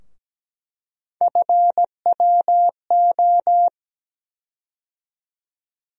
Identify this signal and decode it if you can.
Morse code, heard as a keyed tone, about 17 wpm, 700 Hz. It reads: FWO
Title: USA Call Sign Postfixes - 3 Letters 17wpm